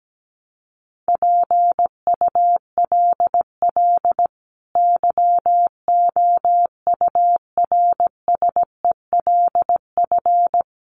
Morse code, heard as a keyed tone, about 17 wpm, 700 Hz. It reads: PULL YOURSELF
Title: Sentences from Top 600 Words 17wpm